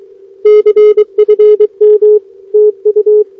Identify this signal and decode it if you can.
Morse code, heard as a keyed tone, about 21 wpm, 420 Hz. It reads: CFM TU
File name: OK QRP 2026 / OKQRP2026